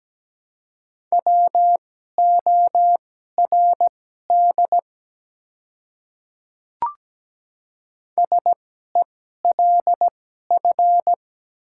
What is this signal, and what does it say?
Morse code, heard as a keyed tone, about 17 wpm, spaced as for 12 wpm, 700 Hz. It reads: WORD SELF
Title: 2x Character Spacing - Top 200 Words - 17wpm